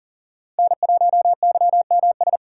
Morse code, heard as a keyed tone, about 40 wpm, 700 Hz. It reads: N1YMS
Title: USA Call Signs - 5 Characters - 3x Character Spacing - 40wpm